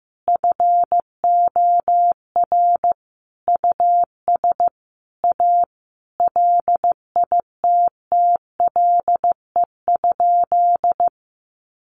Morse code, heard as a keyed tone, about 15 wpm, 700 Hz. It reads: FOR US A LITTLE?
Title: Sentences from Top 900 Words 15wpm